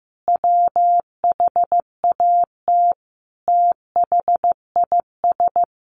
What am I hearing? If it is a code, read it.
Morse code, heard as a keyed tone, about 15 wpm, 700 Hz. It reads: WHAT THIS
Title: Sentences from Top 1500 Words 15wpm